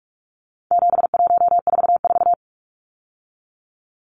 Morse code, heard as a keyed tone, about 44 wpm, 700 Hz. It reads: 7144
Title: Numbers - 4-Digit 45wpm